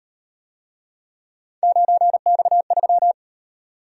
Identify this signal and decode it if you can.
Morse code, heard as a keyed tone, about 38 wpm, 700 Hz. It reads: 9X3